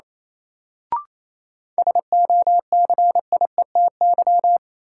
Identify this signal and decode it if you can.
Morse code, heard as a keyed tone, about 28 wpm, 700 Hz. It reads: SOCIETY
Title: Sentences from Top 700 Words - New Word Drill - 28wpm